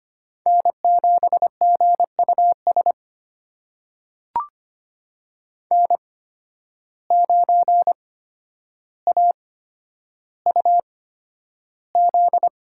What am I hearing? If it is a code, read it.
Morse code, heard as a keyed tone, about 25 wpm, spaced as for 11 wpm, 700 Hz. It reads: N7GUS N9AUZ
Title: USA Call Signs - 5 Characters - 8x Character Spacing - 25wpm